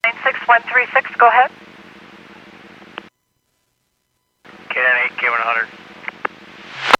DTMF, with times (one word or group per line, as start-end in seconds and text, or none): none